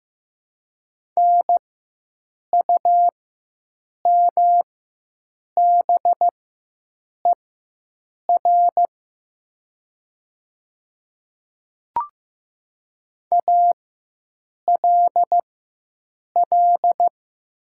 Morse code, heard as a keyed tone, about 15 wpm, spaced as for 7 wpm, 700 Hz. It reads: NUMBER ALL
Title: 4x Character Spacing - Top 100 Words - 15wpm